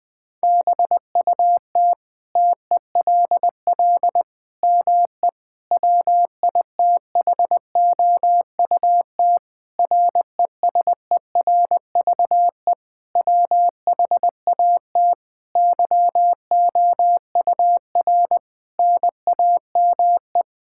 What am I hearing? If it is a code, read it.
Morse code, heard as a keyed tone, about 21 wpm, 700 Hz. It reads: BUT TELL ME WITHOUT RESERVE WHAT YOUR NAME
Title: Sentences from Top 1000 Words - Review Part 6 - 20wpm